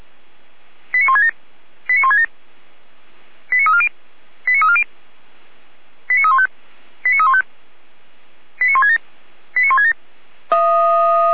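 Five-tone selective call (ZVEI), heard as quick runs of five tones, tones of about 70 ms. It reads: 89167 89167 89340 89340 89315 89315 89167 89167
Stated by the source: ZVEI five-tone call received by SDR, recorded by Feuerwehr Orlamünde fire brigade.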